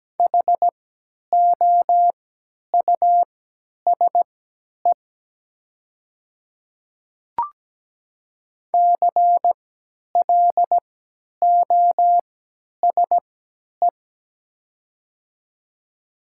Morse code, heard as a keyed tone, about 17 wpm, spaced as for 10 wpm, 700 Hz. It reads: HOUSE CLOSE
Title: 3x Character Spacing - Top 300 Words - 17wpm